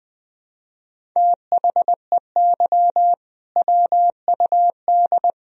Morse code, heard as a keyed tone, about 21 wpm, 700 Hz. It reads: THEY WUD
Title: Sentences from Top 500 Words and Abbreviations 20wpm